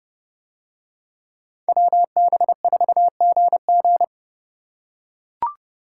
Morse code, heard as a keyed tone, about 28 wpm, 700 Hz. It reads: WB4GG